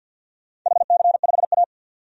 Morse code, heard as a keyed tone, about 50 wpm, 700 Hz. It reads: HC5A
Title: DXCC-100 Call Signs - 4 Characters - 50wpm